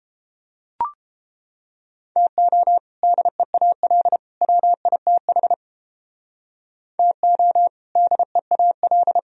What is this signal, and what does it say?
Morse code, heard as a keyed tone, about 31 wpm, 700 Hz. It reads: TO DEAL WITH TO DEAL